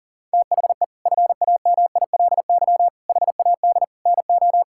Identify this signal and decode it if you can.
Morse code, heard as a keyed tone, about 40 wpm, 700 Hz. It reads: THE FAMILY HAD NO
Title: Sentences from Top 1600 Words 40wpm